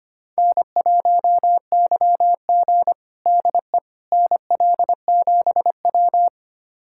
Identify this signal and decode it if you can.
Morse code, heard as a keyed tone, about 25 wpm, 700 Hz. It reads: N1YG DE NL7W
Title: USA Callsign DE Callsign - 4 Characters - 25wpm